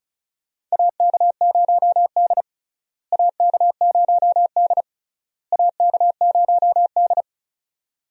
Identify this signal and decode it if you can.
Morse code, heard as a keyed tone, about 34 wpm, 700 Hz. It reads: AK0D AK0D AK0D
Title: USA Call Signs - 4 Characters - Repeated 3x - 35wpm